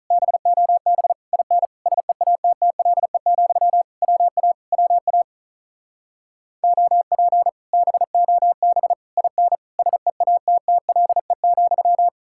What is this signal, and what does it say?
Morse code, heard as a keyed tone, about 44 wpm, 700 Hz. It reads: BOB IN SEATTLE, WA WA OP BOB IN SEATTLE,